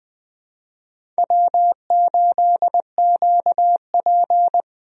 Morse code, heard as a keyed tone, about 20 wpm, 700 Hz. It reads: W8QP
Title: USA Call Signs - 4 Characters - 20wpm